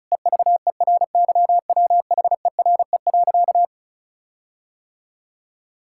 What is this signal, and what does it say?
Morse code, heard as a keyed tone, about 36 wpm, 700 Hz. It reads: EVERYWHERE.
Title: Sentences from Top 1600 Words 35wpm